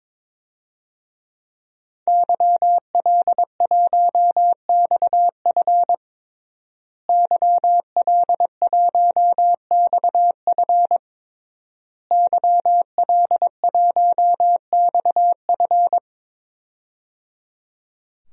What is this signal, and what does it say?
Morse code, heard as a keyed tone, about 22 wpm, 700 Hz. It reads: YL1XF YL1XF YL1XF
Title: DXCC-100 Call Signs - 5 Characters - Repeated 3x - 22wpm